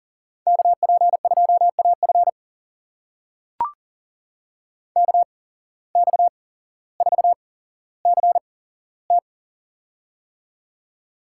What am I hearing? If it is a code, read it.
Morse code, heard as a keyed tone, about 40 wpm, spaced as for 17 wpm, 700 Hz. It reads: KP2AF KX4CT
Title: USA Call Signs - 5 Characters - 8x Character Spacing - 40wpm